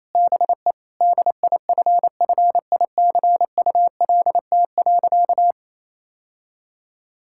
Morse code, heard as a keyed tone, about 28 wpm, 700 Hz. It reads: BE DIFFICULT.
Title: Sentences from Top 1600 Words 28wpm